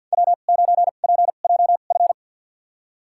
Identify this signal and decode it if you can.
Morse code, heard as a keyed tone, about 51 wpm, spaced as for 36 wpm, 700 Hz. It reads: W9PJF
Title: USA Call Signs - 5 Characters - 2x Character Spacing - 50wpm